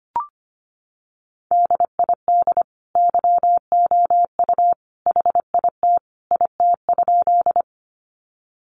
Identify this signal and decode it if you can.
Morse code, heard as a keyed tone, about 25 wpm, 700 Hz. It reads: DID YOU HIT IT?
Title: Sentences from Top 1200 Words 25wpm